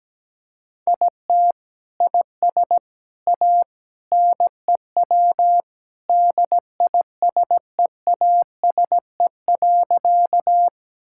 Morse code, heard as a keyed tone, about 17 wpm, 700 Hz. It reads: IT IS A NEW DISEASE.
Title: Sentences from Top 1000 Words - Review Part 6 - 17wpm